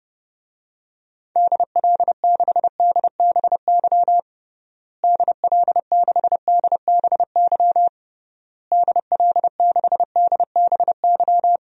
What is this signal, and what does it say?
Morse code, heard as a keyed tone, about 32 wpm, 700 Hz. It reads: DL6DBY DL6DBY DL6DBY
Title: DXCC-100 Call Signs - 6 Characters - Repeated 3x - 30wpm